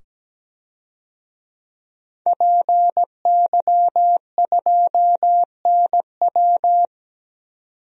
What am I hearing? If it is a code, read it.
Morse code, heard as a keyed tone, about 17 wpm, 700 Hz. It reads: PY2NW
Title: DXCC-100 Call Signs - 5 Characters - 17wpm